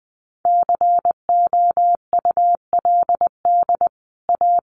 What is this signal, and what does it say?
Morse code, heard as a keyed tone, about 20 wpm, 700 Hz. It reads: COULD A
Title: Sentences from Top 1200 Words 20wpm